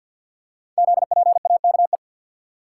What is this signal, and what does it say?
Morse code, heard as a keyed tone, about 48 wpm, 690 Hz. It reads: ZPAKE
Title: Code Groups - 5 Letter:Numbers - 50wpm